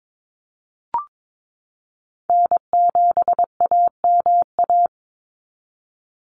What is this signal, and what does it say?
Morse code, heard as a keyed tone, about 22 wpm, 700 Hz. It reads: N7AMA
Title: NA SOTA Chaser Callsigns 22wpm